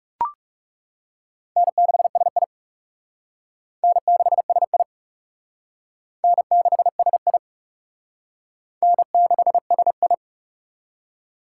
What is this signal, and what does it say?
Morse code, heard as a keyed tone, about 38 wpm, 700 Hz. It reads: N6SI N6SI N6SI N6SI